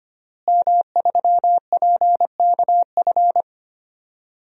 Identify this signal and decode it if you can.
Morse code, heard as a keyed tone, about 25 wpm, 700 Hz. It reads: M3PKF
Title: DXCC-100 Call Signs - 5 Characters - Repeated 3x - 25wpm